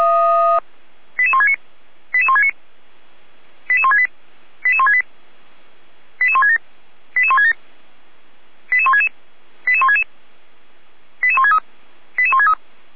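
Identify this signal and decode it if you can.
Five-tone selective call (ZVEI), heard as quick runs of five tones, tones of about 70 ms. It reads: 88169 88169 88168 88168 88167 88167 88166 88166 88163 88163
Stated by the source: Five-tone ZVEI selective call received on SDR.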